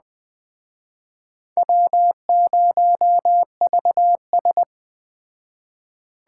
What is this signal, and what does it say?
Morse code, heard as a keyed tone, about 21 wpm, 700 Hz. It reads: W0VS